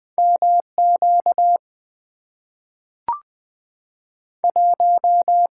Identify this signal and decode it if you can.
Morse code, heard as a keyed tone, about 20 wpm, 700 Hz. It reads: MQ 1
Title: Two Letter-Number Combinations - 20wpm